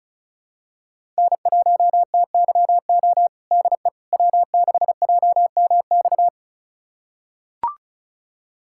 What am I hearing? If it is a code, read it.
Morse code, heard as a keyed tone, about 36 wpm, 700 Hz. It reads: N1TYO DE W6JMX
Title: USA Callsign DE Callsign - 5 Characters - 35wpm